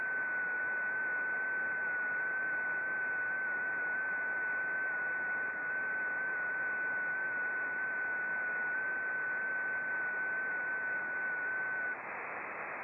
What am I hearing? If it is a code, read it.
FT8.
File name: contests / sept-vhf-k5tr- -0300